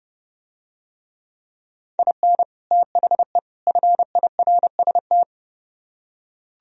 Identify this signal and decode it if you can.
Morse code, heard as a keyed tone, about 30 wpm, 700 Hz. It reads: IN THE FIRST